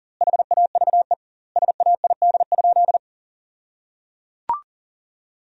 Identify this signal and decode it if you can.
Morse code, heard as a keyed tone, about 37 wpm, 700 Hz. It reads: HAVE SAID?
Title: 2x Word-Spacing - Sentences from Top 100 Words - 40wpm